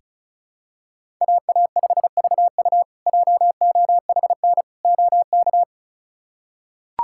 Morse code, heard as a keyed tone, about 36 wpm, 700 Hz. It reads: AA5VU JOHN OK
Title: K1USN SST QSO Practice - 35wpm